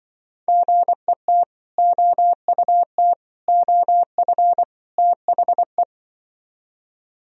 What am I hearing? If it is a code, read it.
Morse code, heard as a keyed tone, about 25 wpm, 700 Hz. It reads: GET OUT OF THE